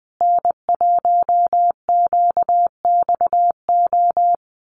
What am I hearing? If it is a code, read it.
Morse code, heard as a keyed tone, about 20 wpm, 700 Hz. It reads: N1QXO
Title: USA Call Signs Repeated 2x 20wpm